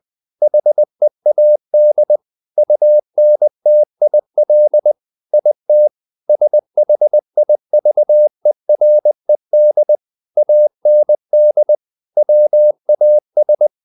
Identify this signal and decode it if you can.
Morse code, heard as a keyed tone, about 20 wpm, 600 Hz. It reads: HEAD UNTIL IT SHIVERED AND WAS
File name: the old man and the sea part 2 20wpm